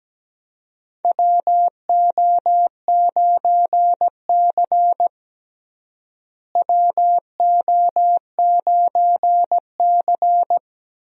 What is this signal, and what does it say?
Morse code, heard as a keyed tone, about 17 wpm, 700 Hz. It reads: WO9C WO9C
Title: USA Call Signs - 4 Characters - Repeated 3x - 17wpm